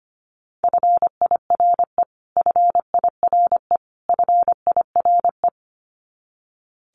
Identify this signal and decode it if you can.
Morse code, heard as a keyed tone, about 25 wpm, 700 Hz. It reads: FIRE FIRE FIRE
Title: Top 500 Words Repeated 25wpm